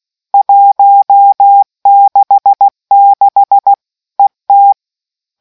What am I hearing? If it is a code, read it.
Morse code, heard as a keyed tone, about 16 wpm, 800 Hz. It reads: 166 ET